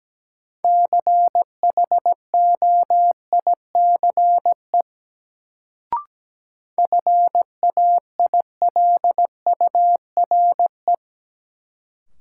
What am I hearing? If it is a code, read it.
Morse code, heard as a keyed tone, about 17 wpm, 700 Hz. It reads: CHOICE FAILURE
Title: Sentences from Top 900 Words - New Word Drill - 17wpm